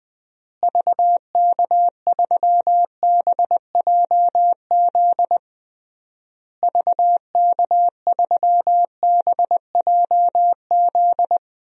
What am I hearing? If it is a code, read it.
Morse code, heard as a keyed tone, about 21 wpm, 700 Hz. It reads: VK3BJZ VK3BJZ
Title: DXCC-100 Call Signs - 6 Characters - Repeated 3x - 20wpm